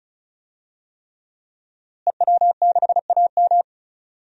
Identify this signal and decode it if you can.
Morse code, heard as a keyed tone, about 36 wpm, 700 Hz. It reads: EW6AM
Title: DXCC-100 Call Signs - 5 Characters - Repeated 3x - 35wpm